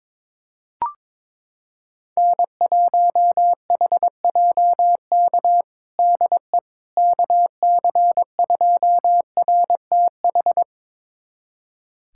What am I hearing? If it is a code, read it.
Morse code, heard as a keyed tone, about 22 wpm, 700 Hz. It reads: N1HJK DE KC2RTH
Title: USA Callsign DE Callsign 22wpm